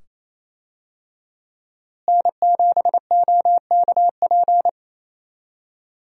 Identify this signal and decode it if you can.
Morse code, heard as a keyed tone, about 28 wpm, 700 Hz. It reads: N7OKP
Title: USA Call Signs - 5 Characters - Repeated 2x - 28wpm